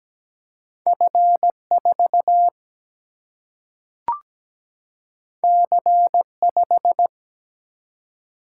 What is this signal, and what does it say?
Morse code, heard as a keyed tone, about 17 wpm, 700 Hz. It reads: F4 C5
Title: Single-Letter Single-Number 17wpm